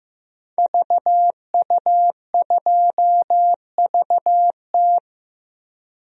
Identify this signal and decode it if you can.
Morse code, heard as a keyed tone, about 15 wpm, 700 Hz. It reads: VU2VT
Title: DXCC-100 Call Signs - 5 Characters - 15wpm